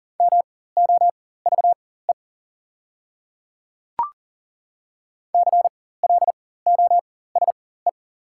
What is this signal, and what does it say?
Morse code, heard as a keyed tone, about 41 wpm, spaced as for 19 wpm, 690 Hz. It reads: MOVE CLOSE
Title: 4x Character Spacing - Top 500 Words - 40wpm